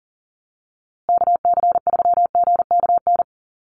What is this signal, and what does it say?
Morse code, heard as a keyed tone, about 40 wpm, 700 Hz. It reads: KC3GKN
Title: USA Call Signs Repeated 2x 40wpm